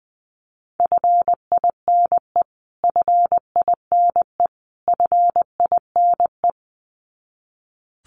Morse code, heard as a keyed tone, about 20 wpm, 700 Hz. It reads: FINE FINE FINE
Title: Top 500 Words Repeated 20wpm